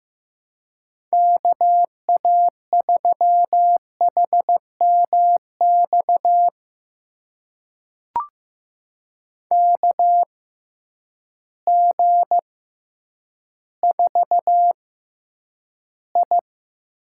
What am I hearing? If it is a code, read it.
Morse code, heard as a keyed tone, about 16 wpm, spaced as for 9 wpm, 700 Hz. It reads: KA3HMX KG4I
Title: USA Call Signs - 6 Characters - 6x Character Spacing - 15wpm